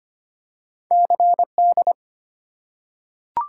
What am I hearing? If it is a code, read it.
Morse code, heard as a keyed tone, about 25 wpm, 690 Hz. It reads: CD
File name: practice / 25wpm / Easy Words with Hints - 25wpm